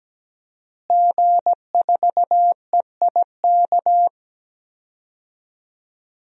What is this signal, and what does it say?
Morse code, heard as a keyed tone, about 17 wpm, 700 Hz. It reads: G4EIK
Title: DXCC-100 Call Signs - 5 Characters - Repeated 2x - 17wpm